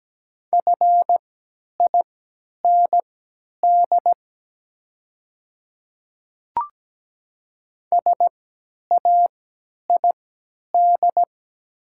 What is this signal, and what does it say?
Morse code, heard as a keyed tone, about 17 wpm, spaced as for 10 wpm, 700 Hz. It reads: FIND SAID
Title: 3x Character Spacing - Top 100 Words - 17wpm